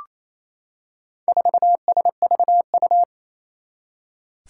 Morse code, heard as a keyed tone, about 27 wpm, 700 Hz. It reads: 4SVU
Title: Four Challenging Letter-Number Combinations 28wpm